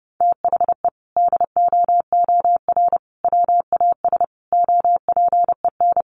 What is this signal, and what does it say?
Morse code, heard as a keyed tone, about 30 wpm, 700 Hz. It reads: THE DOOR WAS OPEN
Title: Sentences from Top 1300 Words 30wpm